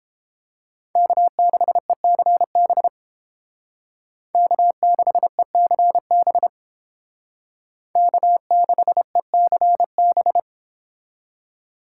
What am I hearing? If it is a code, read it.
Morse code, heard as a keyed tone, about 31 wpm, 700 Hz. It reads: K6ECB K6ECB K6ECB